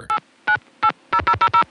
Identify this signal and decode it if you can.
DTMF, heard as telephone touch tones, tones of about 80 ms. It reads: 8980000